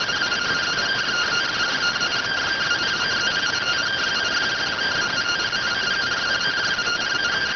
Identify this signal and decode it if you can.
Two-tone teleprinter signal, about 170 Hz shift.